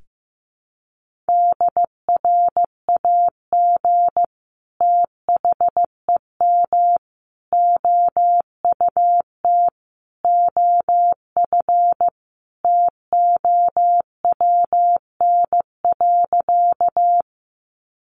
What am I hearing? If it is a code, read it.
Morse code, heard as a keyed tone, about 15 wpm, 700 Hz. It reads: DRAG THEM OUT OF TOWN.
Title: Sentences from Top 1500 Words 15wpm